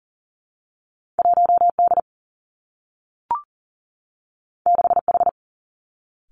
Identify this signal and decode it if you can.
Morse code, heard as a keyed tone, about 38 wpm, 700 Hz. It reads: 1D 6H
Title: Two Challenging Letter-Number Combinations 40wpm